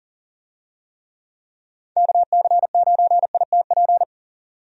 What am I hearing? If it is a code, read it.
Morse code, heard as a keyed tone, about 40 wpm, 700 Hz. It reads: KC9ITP
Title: USA Call Signs - 6 Characters - 40wpm